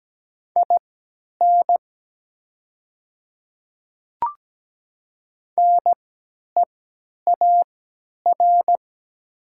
Morse code, heard as a keyed tone, about 17 wpm, spaced as for 10 wpm, 700 Hz. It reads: IN NEAR
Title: 3x Character Spacing - Top 300 Words - 17wpm